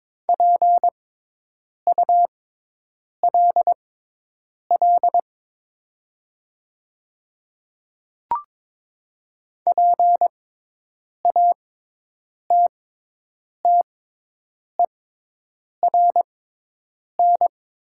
Morse code, heard as a keyed tone, about 22 wpm, spaced as for 8 wpm, 700 Hz. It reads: PULL PATTERN
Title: 6x Character Spacing - Top 500 Words - 22wpm